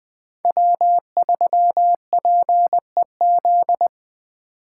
Morse code, heard as a keyed tone, about 20 wpm, 700 Hz. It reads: W3PEZ
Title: USA Call Signs - 2x Character Spacing - 20wpm